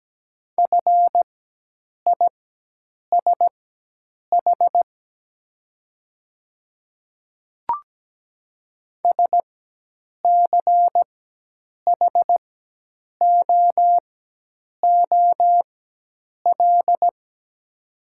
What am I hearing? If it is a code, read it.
Morse code, heard as a keyed tone, about 17 wpm, spaced as for 8 wpm, 700 Hz. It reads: FISH SCHOOL
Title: 4x Character Spacing - Top 500 Words - 17wpm